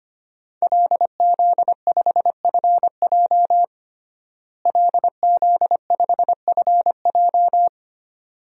Morse code, heard as a keyed tone, about 25 wpm, 700 Hz. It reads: LZ5FJ LZ5FJ
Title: DXCC-100 Call Signs - 5 Characters - Repeated 2x - 25wpm